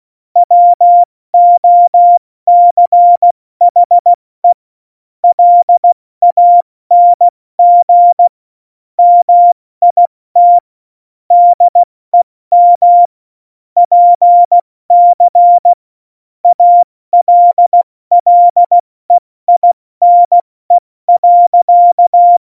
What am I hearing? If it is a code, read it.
Morse code, heard as a keyed tone, about 16 wpm, 700 Hz. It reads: WOCHE LANG MIT DEM PC ALLEINE.
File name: Morsekurs / Witze / Witz 08. Januar 2019, 70 F80